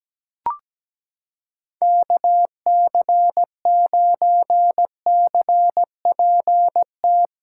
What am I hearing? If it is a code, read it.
Morse code, heard as a keyed tone, about 17 wpm, 700 Hz. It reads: KC9CPT
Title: USA Callsign DE Callsign - 6 Characters - 17wpm